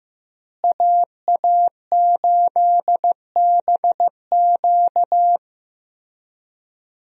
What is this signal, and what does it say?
Morse code, heard as a keyed tone, about 15 wpm, 700 Hz. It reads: AA8BQ
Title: USA Call Signs - 5 Characters - Repeated 3x - 15wpm